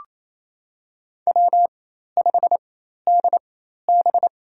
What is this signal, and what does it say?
Morse code, heard as a keyed tone, about 28 wpm, spaced as for 13 wpm, 700 Hz. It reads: W5DB
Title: USA Call Signs - 4 Characters - 4x Character Spacing - 28wpm